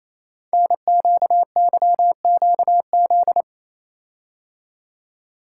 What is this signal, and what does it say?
Morse code, heard as a keyed tone, about 28 wpm, 700 Hz. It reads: NQYQZ
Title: Code Groups - 5 Letters - 28wpm